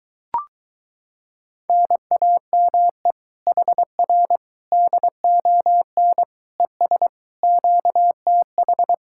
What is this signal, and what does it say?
Morse code, heard as a keyed tone, about 23 wpm, 700 Hz. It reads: NAME HR DON ES QTH